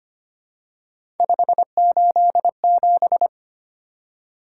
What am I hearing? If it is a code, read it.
Morse code, heard as a keyed tone, about 25 wpm, 700 Hz. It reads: 587
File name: Three Letter-Number Combinations - 25wpm